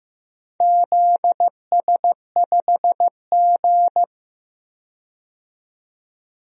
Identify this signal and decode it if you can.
Morse code, heard as a keyed tone, about 15 wpm, 700 Hz. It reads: ZS5G
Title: DXCC-100 Call Signs - 4 Characters - 15wpm